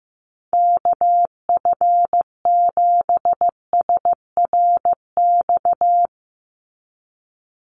Morse code, heard as a keyed tone, about 15 wpm, 700 Hz. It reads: KF7SRX
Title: USA Call Signs 15wpm